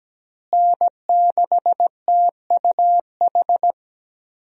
Morse code, heard as a keyed tone, about 17 wpm, 700 Hz. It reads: N6TUH
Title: USA Call Signs - 5 Characters - Repeated 3x - 17wpm